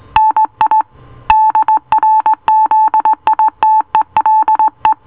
Morse code, heard as a keyed tone, about 18 wpm, 870 Hz. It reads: NI DRZITELE